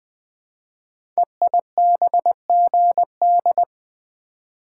Morse code, heard as a keyed tone, about 20 wpm, 700 Hz. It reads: EIBGD
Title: Code Groups - 5 Letters - 20wpm